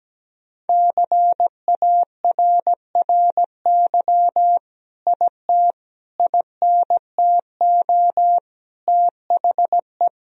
Morse code, heard as a keyed tone, about 17 wpm, 700 Hz. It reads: CARRY IT INTO THE
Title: Sentences from Top 1000 Words - Review Part 1 - 17wpm